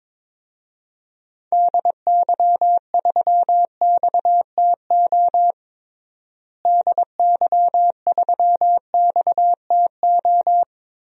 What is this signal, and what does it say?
Morse code, heard as a keyed tone, about 22 wpm, 700 Hz. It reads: DY3XTO DY3XTO
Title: DXCC-100 Call Signs - 6 Characters - Repeated 3x - 22wpm